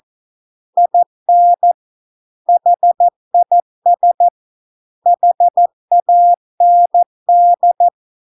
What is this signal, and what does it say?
Morse code, heard as a keyed tone, about 14 wpm, 700 Hz. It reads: IN HIS HAND